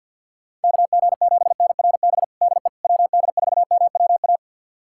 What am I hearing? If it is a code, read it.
Morse code, heard as a keyed tone, about 51 wpm, 700 Hz. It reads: KG7NRB DE WD4MWA
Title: USA Callsign DE Callsign 50wpm